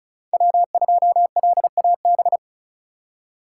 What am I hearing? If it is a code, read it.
Morse code, heard as a keyed tone, about 46 wpm, 700 Hz. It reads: W2LAB